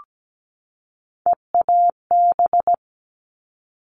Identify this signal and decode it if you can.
Morse code, heard as a keyed tone, about 17 wpm, 700 Hz. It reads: EAB
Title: All Cut Numbers - 3-Digit 17wpm